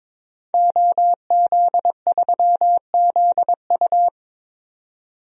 Morse code, heard as a keyed tone, about 22 wpm, 700 Hz. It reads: OZ3ZU
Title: DXCC-100 Call Signs - 5 Characters - 22wpm